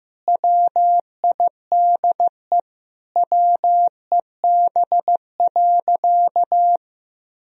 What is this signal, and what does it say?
Morse code, heard as a keyed tone, about 15 wpm, 700 Hz. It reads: WIDE WEB.
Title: Sentences from Top 1000 Words - Review Part 4 - 15wpm